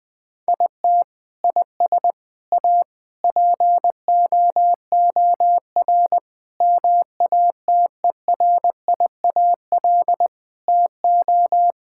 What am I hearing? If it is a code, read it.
Morse code, heard as a keyed tone, about 21 wpm, 700 Hz. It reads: IT IS A POOR MATERIAL TO
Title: Sentences from Top 1000 Words - Review Part 5 - 20wpm